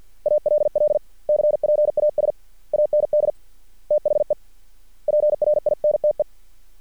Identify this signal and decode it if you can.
Morse code, heard as a keyed tone, about 50 wpm, 600 Hz. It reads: ALL CPAS AND THE PRINTE
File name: archived / CRSnet